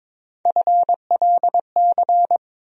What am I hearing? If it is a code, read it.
Morse code, heard as a keyed tone, about 22 wpm, 700 Hz. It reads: FLC
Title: USA Call Sign Postfixes - 3 Letters 22wpm